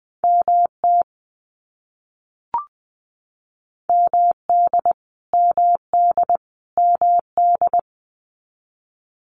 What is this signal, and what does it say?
Morse code, heard as a keyed tone, about 20 wpm, 700 Hz. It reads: MT MD MD MD
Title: US States Repeated 3x 20wpm